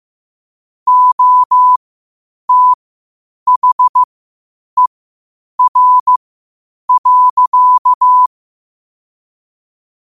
Morse code, heard as a keyed tone, about 15 wpm, spaced as for 8 wpm, 1000 Hz. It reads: OTHER.
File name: qotdfiles / 8wpm